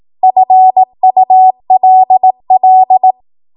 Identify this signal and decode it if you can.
Morse code, heard as a keyed tone, about 18 wpm, 750 Hz. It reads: FULL
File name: Morse / 18WPM